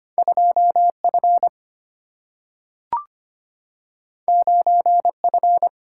Morse code, heard as a keyed tone, about 25 wpm, 700 Hz. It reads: 2 F 9 F
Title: Single-Letter Single-Number 25wpm